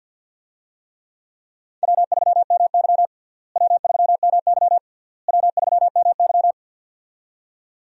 Morse code, heard as a keyed tone, about 52 wpm, 700 Hz. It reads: W3MY W3MY W3MY
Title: USA Call Signs - 4 Characters - Repeated 3x - 50wpm